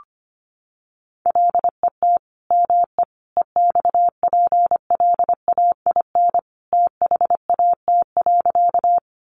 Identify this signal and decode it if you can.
Morse code, heard as a keyed tone, about 25 wpm, 700 Hz. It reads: LET ME EXPLAIN THAT.
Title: Sentences from Top 1100 Words 25wpm